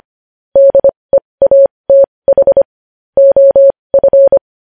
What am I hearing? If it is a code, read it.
Morse code, heard as a keyed tone, about 25 wpm, spaced as for 20 wpm, 550 Hz. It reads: DEATH OF
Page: news-headlines-in-morse-code-at-20-wpm